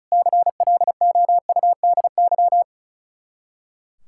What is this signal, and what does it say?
Morse code, heard as a keyed tone, about 44 wpm, 700 Hz. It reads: CLOUDY